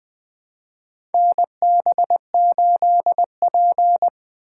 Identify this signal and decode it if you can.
Morse code, heard as a keyed tone, about 21 wpm, 700 Hz. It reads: NB8P
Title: USA Call Signs - 2x Character Spacing - 20wpm